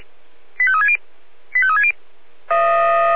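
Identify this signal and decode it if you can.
Five-tone selective call (ZVEI), heard as quick runs of five tones, tones of about 70 ms. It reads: 85360 85360
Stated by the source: ZVEI five-tone selective call 85369 received on an SDR.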